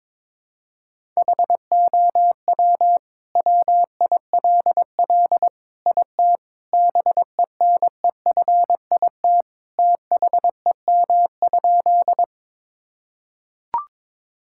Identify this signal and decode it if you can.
Morse code, heard as a keyed tone, about 22 wpm, 700 Hz. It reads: HOW WILL IT BENEFIT THEM?
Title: Sentences from Top 1000 Words - Review Part 6 - 22wpm